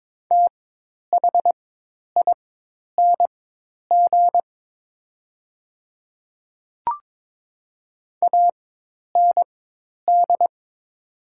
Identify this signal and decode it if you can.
Morse code, heard as a keyed tone, about 22 wpm, spaced as for 10 wpm, 700 Hz. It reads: THING AND
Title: 4x Character Spacing - Top 100 Words - 22wpm